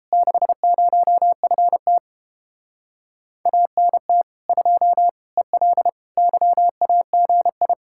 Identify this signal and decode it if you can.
Morse code, heard as a keyed tone, about 31 wpm, 700 Hz. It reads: 60FT ANT 2 EL YAGI